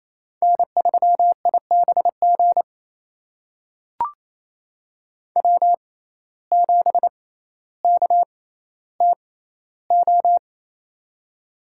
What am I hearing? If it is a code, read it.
Morse code, heard as a keyed tone, about 27 wpm, spaced as for 14 wpm, 700 Hz. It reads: N3IBG W7KTO
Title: USA Call Signs - 5 Characters - 6x Character Spacing - 28wpm